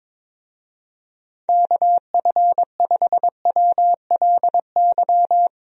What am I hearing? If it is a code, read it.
Morse code, heard as a keyed tone, about 21 wpm, 700 Hz. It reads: KF5WLY
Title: USA Call Signs - 6 Characters - 22wpm